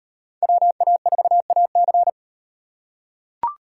Morse code, heard as a keyed tone, about 38 wpm, 700 Hz. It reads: WA4AC